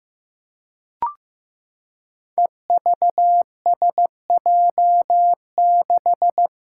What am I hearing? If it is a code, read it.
Morse code, heard as a keyed tone, about 15 wpm, 700 Hz. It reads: EVSJ6
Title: Five Challenging Letter-Number Combinations 15wpm